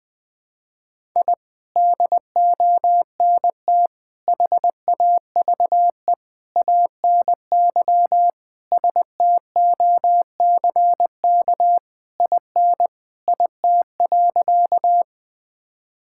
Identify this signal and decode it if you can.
Morse code, heard as a keyed tone, about 20 wpm, 700 Hz. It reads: I DONT HAVE ANY STOCK IN IT.
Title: Sentences from Top 1000 Words - Review Part 5 - 20wpm